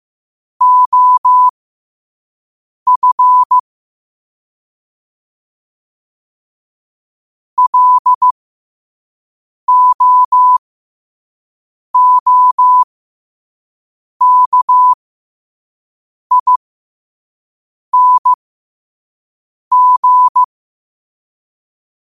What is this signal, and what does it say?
Morse code, heard as a keyed tone, about 15 wpm, spaced as for 5 wpm, 1000 Hz. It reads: OF LOOKING
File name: 2024-10-07-5wpm